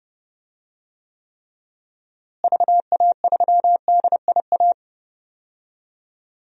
Morse code, heard as a keyed tone, about 30 wpm, 700 Hz. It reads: VA3DIA
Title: DXCC-100 Call Signs - 6 Characters - Repeated 2x - 30wpm